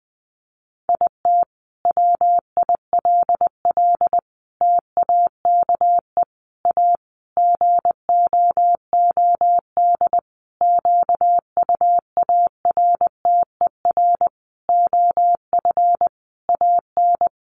Morse code, heard as a keyed tone, about 19 wpm, 700 Hz. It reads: IT WILL TAKE A GOOD QUARTER OF AN
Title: Sentences from Top 1300 Words 20wpm